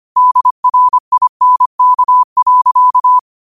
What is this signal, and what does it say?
Morse code, heard as a keyed tone, about 25 wpm, 1000 Hz. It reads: DRINK.
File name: 25wpm